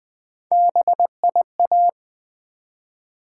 Morse code, heard as a keyed tone, about 20 wpm, 700 Hz. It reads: BIA
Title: Build A Word - 1 Letter At A Time - 4-Letter Words - 20wpm